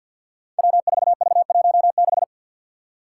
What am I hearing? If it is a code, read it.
Morse code, heard as a keyed tone, about 51 wpm, 700 Hz. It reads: W4V16
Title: Five Challenging Letter-Number Combinations 50wpm